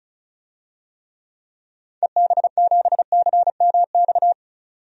Morse code, heard as a keyed tone, about 36 wpm, 700 Hz. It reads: EB7CMX
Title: DXCC-100 Call Signs - 6 Characters - 35wpm